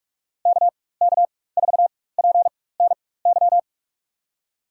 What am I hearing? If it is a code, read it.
Morse code, heard as a keyed tone, about 46 wpm, spaced as for 21 wpm, 700 Hz. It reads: KK4PNY
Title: USA Call Signs - 4x Character Spacing - 45wpm